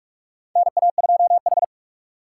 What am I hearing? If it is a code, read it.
Morse code, heard as a keyed tone, about 44 wpm, 700 Hz. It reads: NA2H
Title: USA Call Signs - 4 Characters - 6x Character Spacing - 45wpm